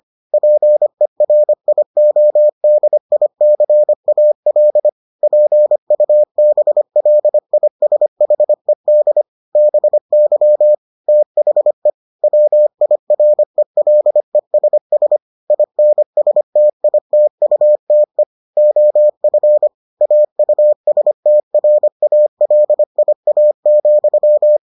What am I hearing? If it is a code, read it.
Morse code, heard as a keyed tone, about 25 wpm, 600 Hz. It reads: PERIODICAL PUBLISHED BY THE WIRELESS INSTITUTE OF AUSTRALIA,